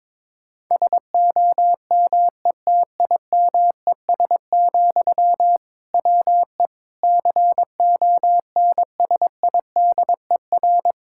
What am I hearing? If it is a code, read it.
Morse code, heard as a keyed tone, about 21 wpm, 700 Hz. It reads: SOMETIMES, WE CONSIDER
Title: Sentences from Top 1000 Words - Review Part 5 - 22wpm